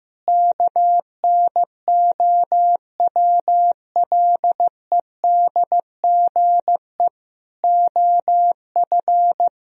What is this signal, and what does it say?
Morse code, heard as a keyed tone, about 15 wpm, 700 Hz. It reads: KNOWLEDGE OF
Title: Sentences from Top 1000 Words - Review Part 3 - 15wpm